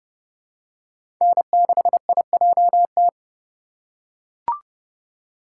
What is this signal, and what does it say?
Morse code, heard as a keyed tone, about 30 wpm, 700 Hz. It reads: N6IJT
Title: USA Call Signs - 5 Characters - 4x Character Spacing - 30wpm